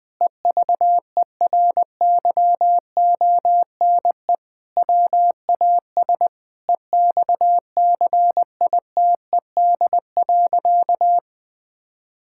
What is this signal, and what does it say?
Morse code, heard as a keyed tone, about 20 wpm, 700 Hz. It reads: EVERYONE WAS EXCITED.
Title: Sentences with 5-Word Phrases - 20wpm